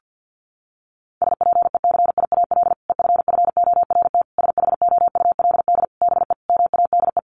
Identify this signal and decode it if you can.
Morse code, heard as a keyed tone, about 49 wpm, 700 Hz. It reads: SPECIAL EFFORT SHOULD BE MADE
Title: Sentences from Top 1200 Words 50wpm